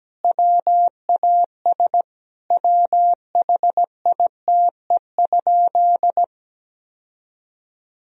Morse code, heard as a keyed tone, about 17 wpm, 700 Hz. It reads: WAS WHITE?
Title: Sentences from Top 1000 Words - Review Part 2 - 17wpm